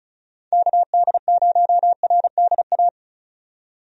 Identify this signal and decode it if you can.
Morse code, heard as a keyed tone, about 36 wpm, 700 Hz. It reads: KD0RDA
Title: USA Call Signs - 6 Characters - 35wpm